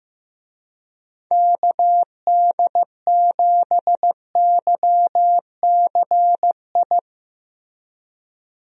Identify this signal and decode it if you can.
Morse code, heard as a keyed tone, about 16 wpm, 700 Hz. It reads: KD7YCI
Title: USA Call Signs - 6 Characters - Repeated 2x - 15wpm